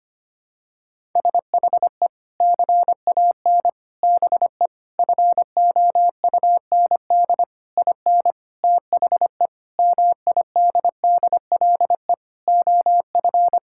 Morse code, heard as a keyed tone, about 25 wpm, 700 Hz. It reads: SHE CAN BE FOUND IN THE MIDDLE OF
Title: Sentences from Top 1000 Words - Review Part 6 - 25wpm